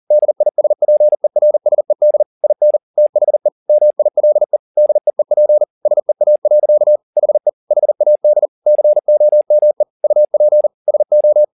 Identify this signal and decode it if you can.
Morse code, heard as a keyed tone, about 40 wpm, 600 Hz. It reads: DISPERSED IN THE MILE DEEP SEA. HE HAD COME UP SO
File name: the old man and the sea 2 40wpm